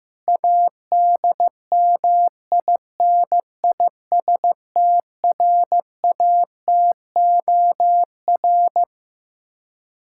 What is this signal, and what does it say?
Morse code, heard as a keyed tone, about 15 wpm, 700 Hz. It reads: ADMINISTRATOR
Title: K1USN SST QSO Practice 15wpm